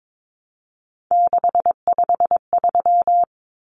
Morse code, heard as a keyed tone, about 22 wpm, 700 Hz. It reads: 653
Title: Numbers - 3-Digit 22wpm